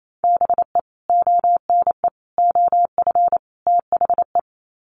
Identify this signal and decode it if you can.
Morse code, heard as a keyed tone, about 28 wpm, 700 Hz. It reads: BE ONE OF THE
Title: Sets of 4 Words 28wpm